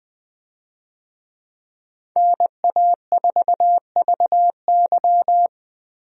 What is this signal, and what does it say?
Morse code, heard as a keyed tone, about 20 wpm, 700 Hz. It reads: NA4VY
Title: USA Call Signs - 5 Characters - Repeated 2x - 20wpm